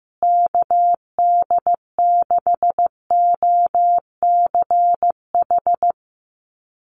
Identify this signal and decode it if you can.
Morse code, heard as a keyed tone, about 15 wpm, 700 Hz. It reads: KD6OCH
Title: USA Call Signs Repeated 3x 15wpm